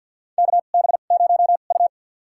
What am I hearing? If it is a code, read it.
Morse code, heard as a keyed tone, about 53 wpm, spaced as for 37 wpm, 700 Hz. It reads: KB0U
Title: USA Call Signs - 4 Characters - 2x Character Spacing - 50wpm